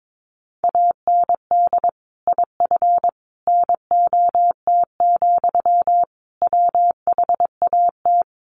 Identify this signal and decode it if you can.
Morse code, heard as a keyed tone, about 22 wpm, 700 Hz. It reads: AND IF NOT, WHAT
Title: Sentences from Top 1400 Words 22wpm